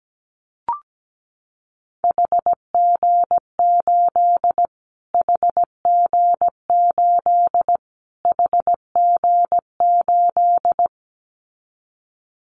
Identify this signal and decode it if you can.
Morse code, heard as a keyed tone, about 17 wpm, 700 Hz. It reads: HG8 HG8 HG8
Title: Callsign Prefixes for Top 25 DXCC - Repeated 3x - 17wpm